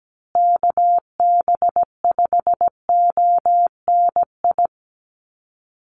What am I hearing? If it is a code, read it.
Morse code, heard as a keyed tone, about 17 wpm, 700 Hz. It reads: KB5ONI
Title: USA Call Signs Repeated 3x 17wpm